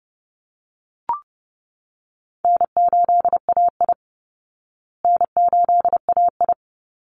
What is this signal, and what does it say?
Morse code, heard as a keyed tone, about 28 wpm, 700 Hz. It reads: N8AI N8AI
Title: Easy Call Signs Repeated 3x - 30wpm